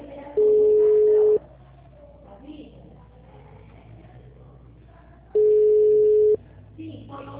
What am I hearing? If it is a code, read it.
Ringback tone (Europe, 425 Hz), with speech.
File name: out- -2011- - - .47288